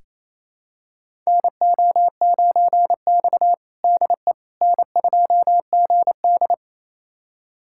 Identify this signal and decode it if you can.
Morse code, heard as a keyed tone, about 27 wpm, 700 Hz. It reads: NO9X DE N2GD
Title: USA Callsign DE Callsign - 4 Characters - 28wpm